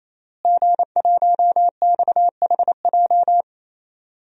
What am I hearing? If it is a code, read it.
Morse code, heard as a keyed tone, about 28 wpm, 700 Hz. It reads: G1XHJ
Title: DXCC-100 Call Signs - 5 Characters - Repeated 3x - 28wpm